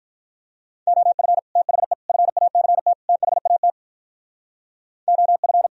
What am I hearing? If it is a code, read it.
Morse code, heard as a keyed tone, about 43 wpm, 700 Hz. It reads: OF THE FACT THAT OF
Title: Speed-Racing - Sets of 4 Words 35wpm